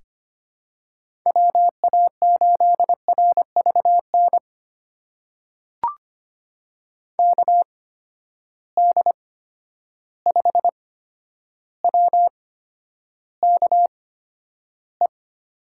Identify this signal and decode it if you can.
Morse code, heard as a keyed tone, about 25 wpm, spaced as for 11 wpm, 700 Hz. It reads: WA8RVN KD5WKE
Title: USA Call Signs - 6 Characters - 8x Character Spacing - 25wpm